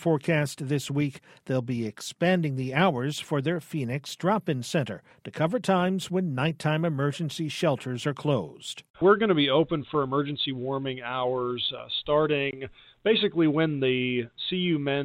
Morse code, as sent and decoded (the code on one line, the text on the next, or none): none